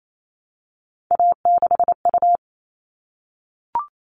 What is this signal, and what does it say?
Morse code, heard as a keyed tone, about 28 wpm, 700 Hz. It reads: A6U
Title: All Cut Numbers - 3-Digit 28wpm